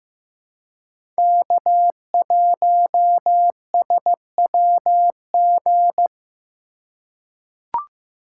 Morse code, heard as a keyed tone, about 15 wpm, 700 Hz. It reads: K1SWG
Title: USA Call Signs - 4x Character Spacing - 15wpm